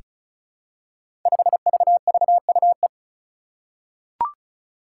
Morse code, heard as a keyed tone, about 34 wpm, 700 Hz. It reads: 5VVUE
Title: Five Challenging Letter-Number Combinations 35wpm